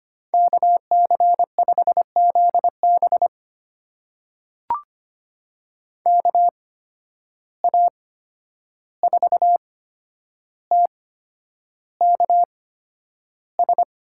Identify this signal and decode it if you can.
Morse code, heard as a keyed tone, about 25 wpm, spaced as for 10 wpm, 700 Hz. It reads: KC5ZB KA4TKS
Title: USA Call Signs - 8x Character Spacing - 25wpm